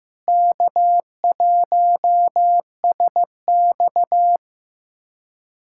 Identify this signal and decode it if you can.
Morse code, heard as a keyed tone, about 15 wpm, 700 Hz. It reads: K1SX
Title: USA Callsign DE Callsign - 4 Characters - 15wpm